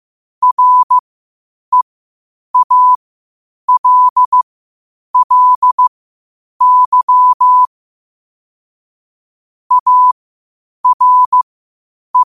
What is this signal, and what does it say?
Morse code, heard as a keyed tone, about 15 wpm, spaced as for 8 wpm, 1000 Hz. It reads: REALLY ARE
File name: qotdfiles / 8wpm